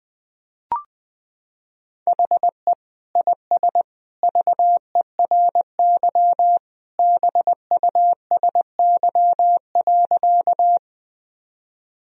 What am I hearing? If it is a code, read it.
Morse code, heard as a keyed tone, about 20 wpm, 700 Hz. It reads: HE IS VERY BUSY.
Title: Sentences from Top 1000 Words - Review Part 3 - 20wpm